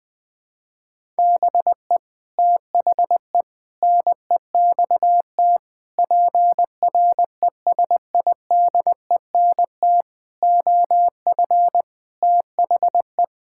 Morse code, heard as a keyed tone, about 21 wpm, 700 Hz. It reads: BE THE NEXT PRESIDENT OF THE
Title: Sets of 8 Words - 20wpm